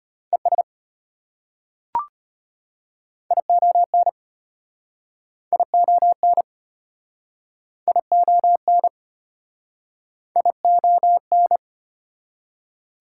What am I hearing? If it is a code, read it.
Morse code, heard as a keyed tone, about 34 wpm, 690 Hz. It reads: ES ION ION ION ION